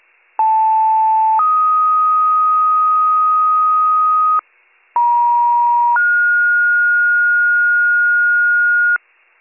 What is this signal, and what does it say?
Two-tone page, heard as a one-second tone, then a three-second tone, about 880 then 1250 Hz, about 950 then 1430 Hz, with noise. It is heard from 0.4 to 9.0 s.